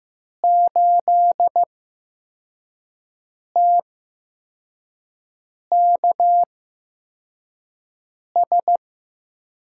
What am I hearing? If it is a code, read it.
Morse code, heard as a keyed tone, about 15 wpm, spaced as for 4 wpm, 700 Hz. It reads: 8TKS
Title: USA Call Signs - 5 Characters - 8x Character Spacing - 15wpm